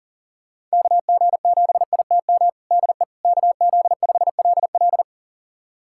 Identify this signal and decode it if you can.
Morse code, heard as a keyed tone, about 40 wpm, 700 Hz. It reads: KG7ITM DE KZ5LL
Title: USA Callsign DE Callsign 40wpm